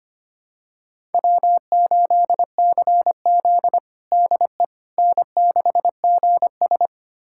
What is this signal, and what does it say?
Morse code, heard as a keyed tone, about 25 wpm, 700 Hz. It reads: W8CZ DE N6GS
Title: USA Callsign DE Callsign - 4 Characters - 25wpm